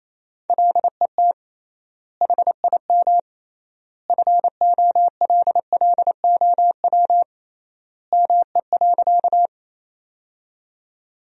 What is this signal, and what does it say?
Morse code, heard as a keyed tone, about 28 wpm, 700 Hz. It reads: LET HIM FOLLOW ME.
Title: 3x Word-Spacing - Sentences from Top 300 Words - 28wpm